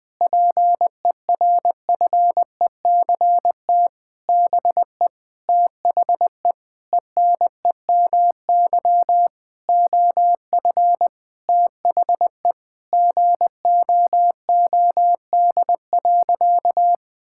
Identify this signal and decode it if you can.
Morse code, heard as a keyed tone, about 20 wpm, 700 Hz. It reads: PERFECT BE THE ENEMY OF THE GOOD.
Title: Sentences with 9-Word Phrases - 20wpm